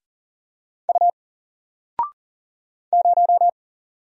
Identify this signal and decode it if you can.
Morse code, heard as a keyed tone, about 40 wpm, 700 Hz. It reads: U 0
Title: Single Character (Letter:Number:Punctuation:Prosign) Alt 40wpm